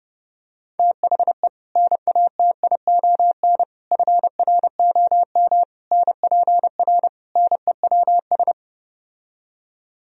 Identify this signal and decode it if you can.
Morse code, heard as a keyed tone, about 30 wpm, 700 Hz. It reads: THE NATION FROM NPR NEWS